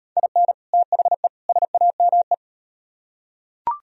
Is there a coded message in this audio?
Yes: Morse code from 0.2 to 2.3 s, heard as a keyed tone, about 38 wpm, 700 Hz. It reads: IN THE SAME